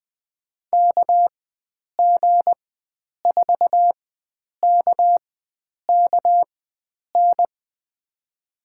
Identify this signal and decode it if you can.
Morse code, heard as a keyed tone, about 21 wpm, spaced as for 9 wpm, 700 Hz. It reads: KG4KKN
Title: USA Call Signs - 6 Characters - 4x Character Spacing - 20wpm